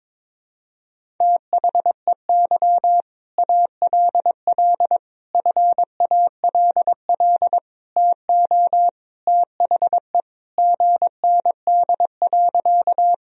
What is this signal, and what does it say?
Morse code, heard as a keyed tone, about 22 wpm, 700 Hz. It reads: THEY ALL FALL TO THE GND.
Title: Sentences from Top 500 Words and Abbreviations 22wpm